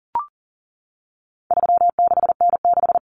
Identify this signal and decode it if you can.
Morse code, heard as a keyed tone, about 40 wpm, 700 Hz. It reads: 36N6
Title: Common Cut Numbers - 4-Digit 40wpm